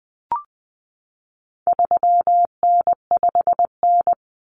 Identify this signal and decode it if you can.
Morse code, heard as a keyed tone, about 20 wpm, 700 Hz. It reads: 3N5N
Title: Common Cut Numbers - 4-Digit 20wpm